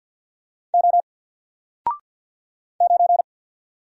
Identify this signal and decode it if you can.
Morse code, heard as a keyed tone, about 50 wpm, 700 Hz. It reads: O 9
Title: Single Letter-Number 50wpm